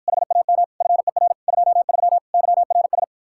Morse code, heard as a keyed tone, about 54 wpm, 700 Hz. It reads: HAM FER 23 YRS